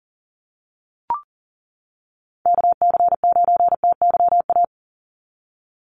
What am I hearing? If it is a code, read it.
Morse code, heard as a keyed tone, about 40 wpm, 700 Hz. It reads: KC9TYA
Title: NA SOTA Chaser Callsigns 40wpm